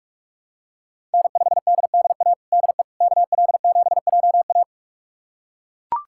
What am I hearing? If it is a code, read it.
Morse code, heard as a keyed tone, about 42 wpm, 700 Hz. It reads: N5DDA DE KL7JA
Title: USA Callsign DE Callsign - 5 Characters - 45wpm